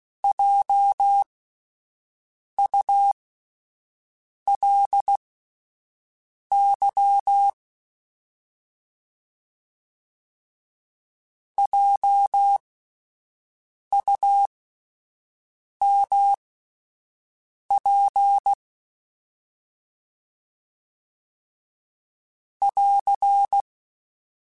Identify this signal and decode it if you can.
Morse code, heard as a keyed tone, about 16 wpm, spaced as for 5 wpm, 800 Hz. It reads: JULY JUMP <AR>